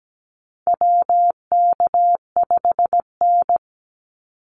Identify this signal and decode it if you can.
Morse code, heard as a keyed tone, about 17 wpm, 700 Hz. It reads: WK5N
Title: Call Signs - 17wpm